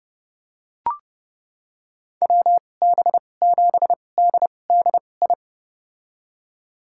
Morse code, heard as a keyed tone, about 32 wpm, spaced as for 22 wpm, 700 Hz. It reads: WB7DDI
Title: USA Call Signs - 2x Character Spacing - 30wpm